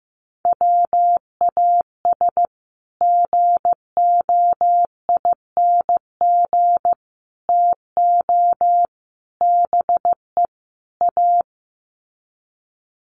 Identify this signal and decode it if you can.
Morse code, heard as a keyed tone, about 15 wpm, 700 Hz. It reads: WAS GOING TO BE A
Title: Sets of 5 Words 15wpm